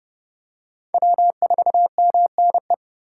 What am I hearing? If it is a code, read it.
Morse code, heard as a keyed tone, about 28 wpm, 700 Hz. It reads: W4MNE